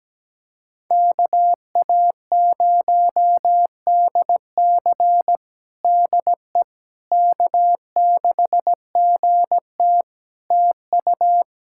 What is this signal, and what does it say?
Morse code, heard as a keyed tone, about 17 wpm, 700 Hz. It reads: KA0DC DE K6GT TU
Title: Practice QSO - Exchange 4 - Ending - 17wpm